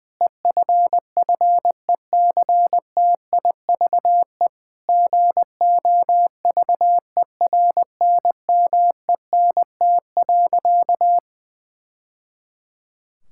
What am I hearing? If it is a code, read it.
Morse code, heard as a keyed tone, about 20 wpm, 700 Hz. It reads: EFFECTIVE GOVERNMENT.
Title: Sentences from Top 1000 Words - Review Part 6 - 20wpm